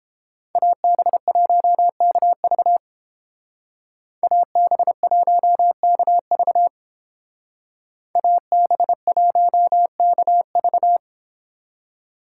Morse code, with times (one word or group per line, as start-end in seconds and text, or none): 0.6-2.8 AB1KV
4.2-6.7 AB1KV
8.2-11.0 AB1KV